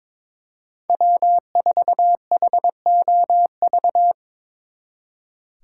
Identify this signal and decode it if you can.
Morse code, heard as a keyed tone, about 24 wpm, 700 Hz. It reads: W4HOV
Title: Speed Racing - USA 1x3 Call Signs 22wpm